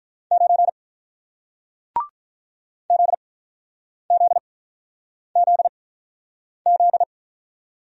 Morse code, heard as a keyed tone, about 48 wpm, spaced as for 9 wpm, 700 Hz. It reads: 9 ZZZZ